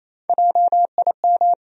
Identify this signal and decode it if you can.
Morse code, heard as a keyed tone, about 28 wpm, 700 Hz. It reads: JIM
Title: USA Common Names - Max 3 Letters - 28wpm